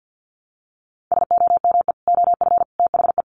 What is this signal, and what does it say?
Morse code, heard as a keyed tone, about 50 wpm, 700 Hz. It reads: SOME OF THE
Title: Sets of 3 Words 50wpm